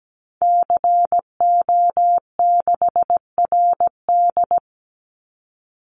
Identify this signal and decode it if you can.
Morse code, heard as a keyed tone, about 17 wpm, 700 Hz. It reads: CO6RD
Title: Call Signs Repeated 2x - 17wpm